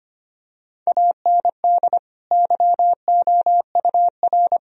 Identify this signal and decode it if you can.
Morse code, heard as a keyed tone, about 25 wpm, 700 Hz. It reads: AND YOUR
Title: Sentences from Top 1000 Words - Review Part 4 - 25wpm